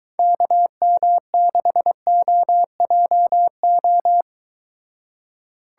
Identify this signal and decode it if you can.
Morse code, heard as a keyed tone, about 23 wpm, 700 Hz. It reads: KM6OJO